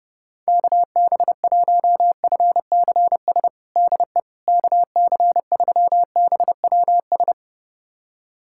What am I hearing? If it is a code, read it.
Morse code, heard as a keyed tone, about 30 wpm, 700 Hz. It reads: KB1FCS DE KC3BWS
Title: USA Callsign DE Callsign 30wpm